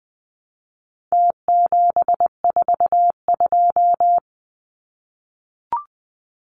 Morse code, heard as a keyed tone, about 20 wpm, 700 Hz. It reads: T742
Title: Numbers - Contest Style - 4-Digit 20wpm